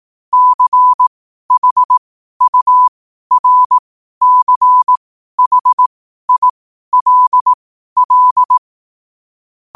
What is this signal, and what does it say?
Morse code, heard as a keyed tone, about 18 wpm, spaced as for 12 wpm, 1000 Hz. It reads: CHURCHILL
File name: qotdfiles / 12wpm